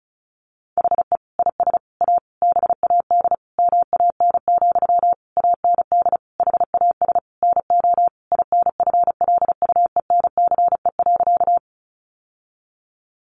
Morse code, heard as a keyed tone, about 34 wpm, 700 Hz. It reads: HE IS A BAD MAN, AND HAS NO INFLUENCE.
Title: Sentences from Top 1100 Words 35wpm